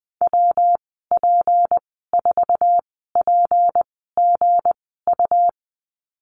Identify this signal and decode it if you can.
Morse code, heard as a keyed tone, about 19 wpm, spaced as for 14 wpm, 700 Hz. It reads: WP4PGU
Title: 193 Lesson 31g - Letter v - Call Signs 20wpm